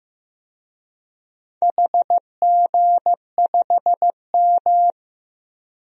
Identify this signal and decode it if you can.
Morse code, heard as a keyed tone, about 15 wpm, 700 Hz. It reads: HG5M
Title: DXCC-100 Call Signs - 4 Characters - Repeated 2x - 15wpm